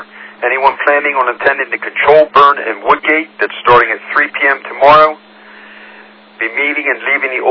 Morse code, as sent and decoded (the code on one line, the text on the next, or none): none